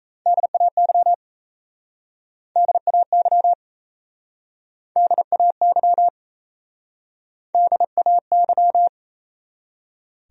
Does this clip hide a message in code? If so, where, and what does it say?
Yes: Morse code from 0.3 to 8.9 s, heard as a keyed tone, about 39 wpm, 700 Hz. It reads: DAY DAY DAY DAY